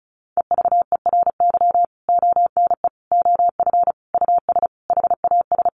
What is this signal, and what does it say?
Morse code, heard as a keyed tone, about 34 wpm, 700 Hz. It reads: EVERY ONE OF US HAS
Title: Sentences from Top 1500 Words 35wpm